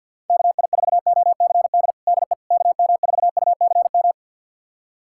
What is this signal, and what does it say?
Morse code, heard as a keyed tone, about 45 wpm, 690 Hz. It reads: KI4OCD DE KM4UCM
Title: USA Callsign DE Callsign - 6 Characters - 50wpm